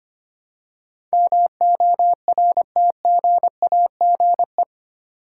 Morse code, heard as a keyed tone, about 25 wpm, 700 Hz. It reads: MORTGAGE
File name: 8 Letter Words - First 3 Letters Given - 25wpm